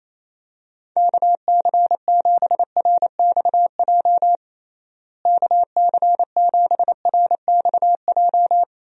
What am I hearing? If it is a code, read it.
Morse code, heard as a keyed tone, about 28 wpm, 700 Hz. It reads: KC7RXJ KC7RXJ
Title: USA Call Signs - 6 Characters - Repeated 3x - 28wpm